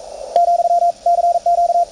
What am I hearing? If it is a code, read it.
Morse code, heard as a keyed tone, about 44 wpm, 690 Hz, with radio noise. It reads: , CQ
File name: attachments / CW 20m 200Hz